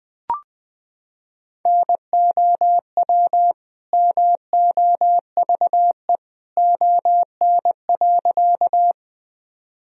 Morse code, heard as a keyed tone, about 20 wpm, 700 Hz. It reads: NOW MOVE ON.
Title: Sentences from Top 1000 Words - Review Part 1 - 20wpm